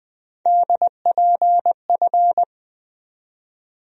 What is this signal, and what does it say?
Morse code, heard as a keyed tone, about 20 wpm, 700 Hz. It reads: DPF
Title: USA Call Sign Postfixes - 3 Letters 20wpm